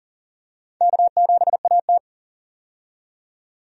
Morse code, heard as a keyed tone, about 40 wpm, 700 Hz. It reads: K7AT
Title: USA Call Signs - 4 Characters - 40wpm